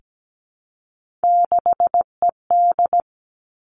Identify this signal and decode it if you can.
Morse code, heard as a keyed tone, about 17 wpm, 700 Hz. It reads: 6ED
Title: Three Challenging Letter-Number Combinations 17wpm